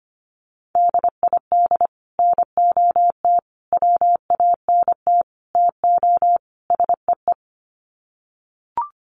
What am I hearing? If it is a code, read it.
Morse code, heard as a keyed tone, about 25 wpm, 700 Hz. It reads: DID NOT WANT TO SEE
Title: Sets of 5 Words Encore2 25wpm